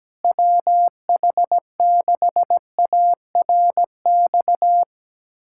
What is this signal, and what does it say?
Morse code, heard as a keyed tone, about 17 wpm, 700 Hz. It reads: WH6ARX
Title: USA Callsign DE Callsign - 6 Characters - 17wpm